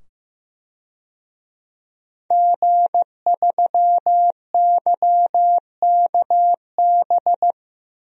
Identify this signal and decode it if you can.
Morse code, heard as a keyed tone, about 15 wpm, 700 Hz. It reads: G3YKB
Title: DXCC-100 Call Signs - 5 Characters - 15wpm